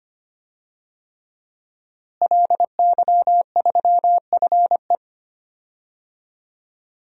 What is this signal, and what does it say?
Morse code, heard as a keyed tone, about 25 wpm, 700 Hz. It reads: LY3FE
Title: DXCC-100 Call Signs - 5 Characters - Repeated 3x - 25wpm